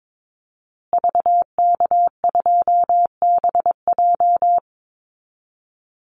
Morse code, heard as a keyed tone, about 22 wpm, 700 Hz. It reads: VK2BJ
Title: Call Signs Repeated 2x - 22wpm